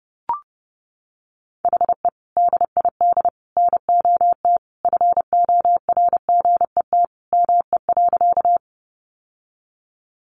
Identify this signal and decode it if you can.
Morse code, heard as a keyed tone, about 30 wpm, 700 Hz. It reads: HE DID NOT FORGET ME.
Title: Sentences from Top 1200 Words 30wpm